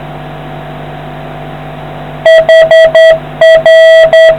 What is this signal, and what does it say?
Morse code, heard as a keyed tone, about 10 wpm, 650 Hz. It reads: HR